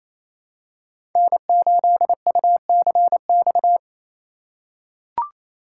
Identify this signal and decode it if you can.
Morse code, heard as a keyed tone, about 28 wpm, 700 Hz. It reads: N8UCX
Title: USA Call Signs - 5 Characters - 2x Character Spacing - 28wpm